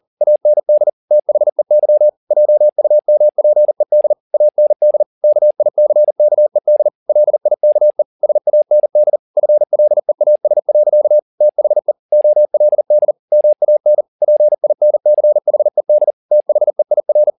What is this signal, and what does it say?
Morse code, heard as a keyed tone, about 40 wpm, 600 Hz. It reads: AND THEY JUMPED AND KICKED LIKE SAND FLEAS. THE OLD MAN PINCHED THEIR
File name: the old man and the sea 2 40wpm